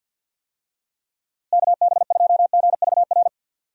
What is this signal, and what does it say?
Morse code, heard as a keyed tone, about 50 wpm, 700 Hz. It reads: KB2GVR
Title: USA Call Signs - 6 Characters - Repeated 3x - 50wpm